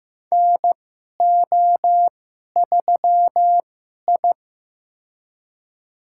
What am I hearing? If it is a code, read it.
Morse code, heard as a keyed tone, about 15 wpm, spaced as for 11 wpm, 700 Hz. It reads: NO3I
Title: USA Call Signs - 2x Character Spacing - 15wpm